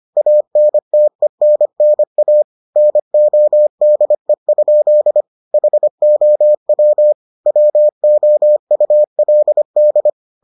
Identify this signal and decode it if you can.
Morse code, heard as a keyed tone, about 25 wpm, 600 Hz. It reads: ANTENNA NODE? HOW WOULD